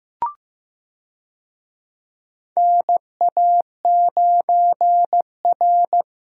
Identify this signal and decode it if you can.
Morse code, heard as a keyed tone, about 15 wpm, 700 Hz. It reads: NA9R
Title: USA Call Signs - 4 Characters - Repeated 2x - 15wpm